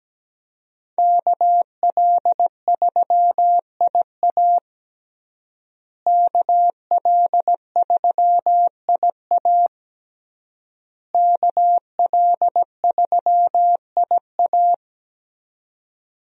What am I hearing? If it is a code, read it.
Morse code, heard as a keyed tone, about 17 wpm, 700 Hz. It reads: KL3IA KL3IA KL3IA
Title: USA Call Signs - 5 Characters - Repeated 3x - 17wpm